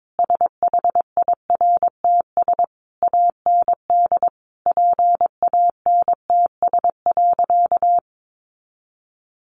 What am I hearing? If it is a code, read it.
Morse code, heard as a keyed tone, about 22 wpm, 700 Hz. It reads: SHIRTS AND PANTS.
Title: A Sound Of Thunder 22wpm